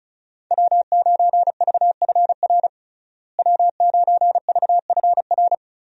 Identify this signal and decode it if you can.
Morse code, heard as a keyed tone, about 36 wpm, 700 Hz. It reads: W9VFR W9VFR
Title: USA Call Signs - 5 Characters - Repeated 3x - 35wpm